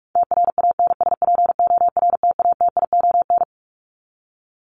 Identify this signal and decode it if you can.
Morse code, heard as a keyed tone, about 44 wpm, 700 Hz. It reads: TRANSPORTATION
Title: Top 1501-1750 Words 45wpm